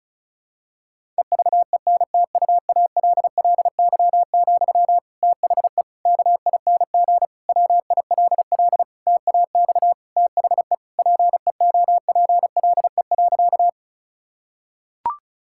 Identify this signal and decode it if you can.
Morse code, heard as a keyed tone, about 36 wpm, 700 Hz. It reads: EVENTUALLY, THE KING WILL TAX THE PEOPLE.
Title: Sentences from Top 1000 Words - Review Part 6 - 35wpm